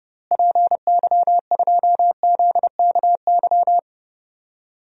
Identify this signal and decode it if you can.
Morse code, heard as a keyed tone, about 32 wpm, 700 Hz. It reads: PY2ZKY
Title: DXCC-100 Call Signs - 6 Characters - Repeated 3x - 30wpm